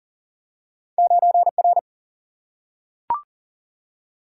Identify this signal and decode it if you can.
Morse code, heard as a keyed tone, about 38 wpm, 700 Hz. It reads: 9R
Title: Single-Letter Single-Number 40wpm